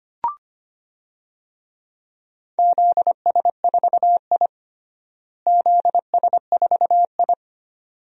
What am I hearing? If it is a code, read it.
Morse code, heard as a keyed tone, about 25 wpm, 700 Hz. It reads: ZS4I ZS4I
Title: DXCC-100 Call Signs - 4 Characters - Repeated 2x - 25wpm